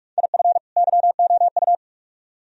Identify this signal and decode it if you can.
Morse code, heard as a keyed tone, about 44 wpm, 700 Hz. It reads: IF YOU